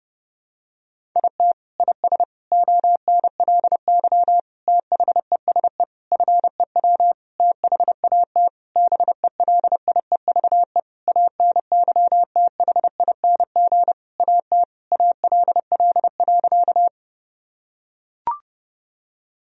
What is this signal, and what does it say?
Morse code, heard as a keyed tone, about 32 wpm, 700 Hz. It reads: IT IS ONLY THESE FEW THAT BELIEVE ANYTHING AT ALL.
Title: Sentences from Top 1000 Words - Review Part 1 - 30wpm